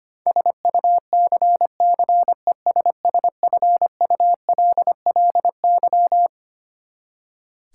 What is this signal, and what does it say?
Morse code, heard as a keyed tone, about 25 wpm, 700 Hz. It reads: SUCCESSFULLY
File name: practice / 25wpm / Twelve Letter Words - 25wpm